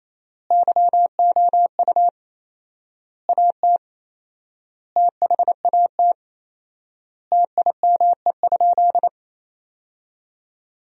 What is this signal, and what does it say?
Morse code, heard as a keyed tone, about 28 wpm, 700 Hz. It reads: YOU AT THAT TIME?
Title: 4x Word-Spacing - Sentences from Top 200 Words - 28wpm